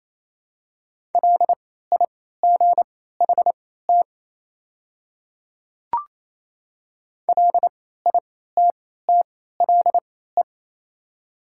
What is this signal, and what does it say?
Morse code, heard as a keyed tone, about 28 wpm, spaced as for 16 wpm, 700 Hz. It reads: LIGHT LITTLE
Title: 3x Character Spacing - Top 200 Words - 28wpm